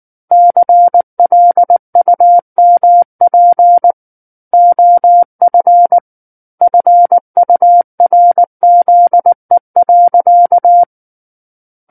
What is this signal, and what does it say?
Morse code, heard as a keyed tone, about 19 wpm, 700 Hz. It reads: CLUMP OF FURZE.
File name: Book One / 10-Chapter 10 - 19 WPM